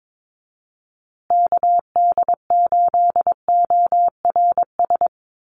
Morse code, heard as a keyed tone, about 22 wpm, 700 Hz. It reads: KD8ORS
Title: USA Call Signs Repeated 2x 22wpm